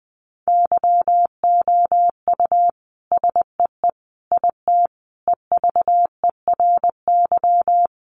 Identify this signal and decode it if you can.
Morse code, heard as a keyed tone, about 19 wpm, 700 Hz. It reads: YOU SEE IT EVERY
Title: Sentences from Top 200 Words 20wpm